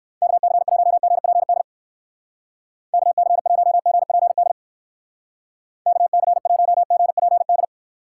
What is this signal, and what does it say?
Morse code, heard as a keyed tone, about 56 wpm, 700 Hz. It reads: KC1GPD KC1GPD KC1GPD